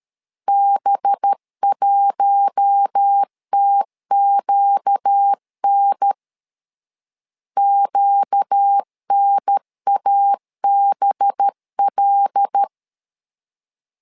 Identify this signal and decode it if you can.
Morse code, heard as a keyed tone, about 13 wpm, 800 Hz. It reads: B1TQN QNABL
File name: LEZIONI / 5 LBVC / 5 LB 12wpm 1906